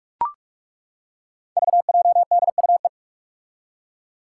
Morse code, heard as a keyed tone, about 41 wpm, 700 Hz. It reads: VJDUE
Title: Five Challenging Letter-Number Combinations 45wpm